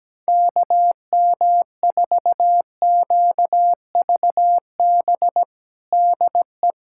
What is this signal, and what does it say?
Morse code, heard as a keyed tone, about 17 wpm, 700 Hz. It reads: KM4QVB DE
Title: USA Callsign DE Callsign - 6 Characters - 17wpm